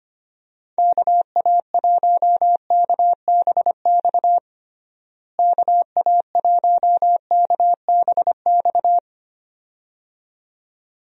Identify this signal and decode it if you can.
Morse code, heard as a keyed tone, about 25 wpm, 700 Hz. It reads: KA1KBX KA1KBX
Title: USA Call Signs - 6 Characters - Repeated 2x - 25wpm